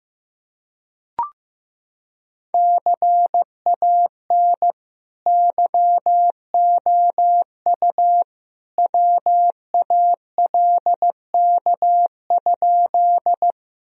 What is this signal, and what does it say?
Morse code, heard as a keyed tone, about 16 wpm, 700 Hz. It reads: CAN YOU WALK?
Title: Sentences from Top 1000 Words - Review Part 1 - 15wpm